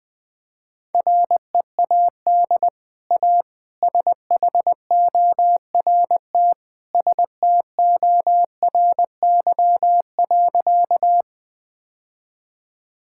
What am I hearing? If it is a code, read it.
Morse code, heard as a keyed tone, about 20 wpm, 700 Hz. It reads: READ A SHORT STORY.
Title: Easy Sentences with Hints - 20wpm